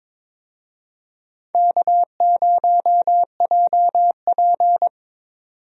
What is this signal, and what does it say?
Morse code, heard as a keyed tone, about 21 wpm, 700 Hz. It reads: K0JP
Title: USA Call Signs - 4 Characters - 22wpm